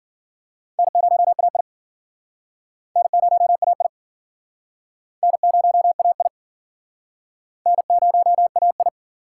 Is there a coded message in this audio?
Yes: Morse code from 0.8 to 8.9 s, heard as a keyed tone, about 53 wpm, 700 Hz. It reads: N0AI N0AI N0AI N0AI